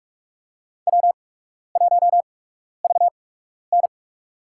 Morse code, heard as a keyed tone, about 44 wpm, spaced as for 12 wpm, 700 Hz. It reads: W1VN
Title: USA Call Signs - 4 Characters - 8x Character Spacing - 45wpm